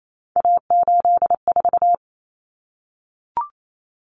Morse code, heard as a keyed tone, about 28 wpm, 700 Hz. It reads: A84
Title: Common Cut Numbers - 3-Digit 28wpm